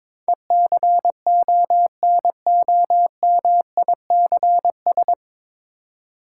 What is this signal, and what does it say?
Morse code, heard as a keyed tone, about 22 wpm, 700 Hz. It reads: ECONOMICS
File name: 9 Letter Words - First 3 Letters Given - 22wpm